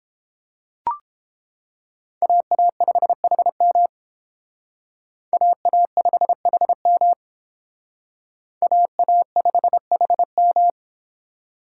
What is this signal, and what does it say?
Morse code, heard as a keyed tone, about 31 wpm, 700 Hz. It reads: AA5HM AA5HM AA5HM